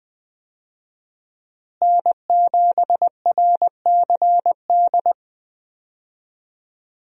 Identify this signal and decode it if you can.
Morse code, heard as a keyed tone, about 21 wpm, 700 Hz. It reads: N7RCD
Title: USA Call Signs - 5 Characters - Repeated 3x - 20wpm